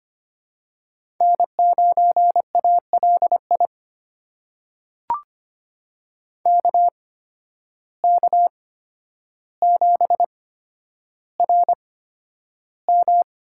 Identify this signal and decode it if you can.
Morse code, heard as a keyed tone, about 25 wpm, spaced as for 11 wpm, 700 Hz. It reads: N9ALI KK7RM
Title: USA Call Signs - 8x Character Spacing - 25wpm